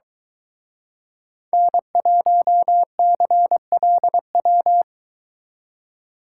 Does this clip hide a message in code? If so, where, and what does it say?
Yes: Morse code from 1.5 to 4.8 s, heard as a keyed tone, about 23 wpm, 700 Hz. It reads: N1CLW